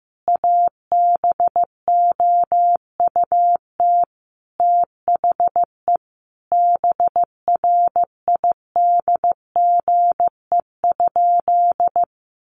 Morse code, heard as a keyed tone, about 15 wpm, 700 Hz. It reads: ABOUT THE BRIDGE?
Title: Sentences from Top 1200 Words 15wpm